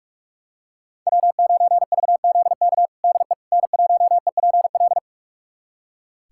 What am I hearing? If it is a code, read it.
Morse code, heard as a keyed tone, about 43 wpm, 700 Hz. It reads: W9VZK DE N1EPL
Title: USA Callsign DE Callsign - 5 Characters - 45wpm